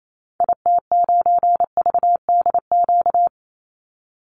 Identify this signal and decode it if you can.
Morse code, heard as a keyed tone, about 28 wpm, 700 Hz. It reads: IT9VDQ
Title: Call Signs - 28wpm